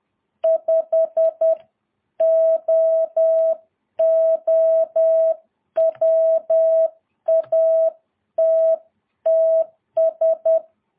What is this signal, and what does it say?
Morse code, heard as a keyed tone, about 10 wpm, 650 Hz. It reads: 5OO WATTS
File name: jh8cbh / 2022cw